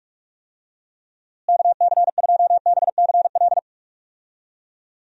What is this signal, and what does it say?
Morse code, heard as a keyed tone, about 46 wpm, 700 Hz. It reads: KC2BCL
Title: USA Call Signs - 6 Characters - Repeated 2x - 45wpm